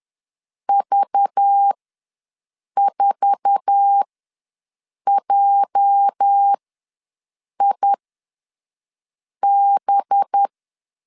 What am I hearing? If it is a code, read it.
Morse code, heard as a keyed tone, about 10 wpm, spaced as for 5 wpm, 800 Hz. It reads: V4JIB
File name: LEZIONI / 8 YJ45 / 8 YJ4 ws 10wpm 4020